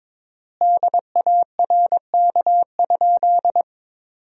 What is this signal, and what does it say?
Morse code, heard as a keyed tone, about 22 wpm, 700 Hz. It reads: DARK?
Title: 4x Word-Spacing - Sentences from Top 500 Words - 22wpm